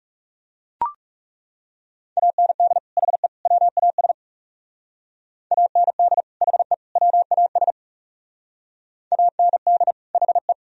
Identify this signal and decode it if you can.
Morse code, heard as a keyed tone, about 35 wpm, 700 Hz. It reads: AND HE WAS AND HE WAS AND HE
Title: Speed-Racing - Sets of 3 Words 30wpm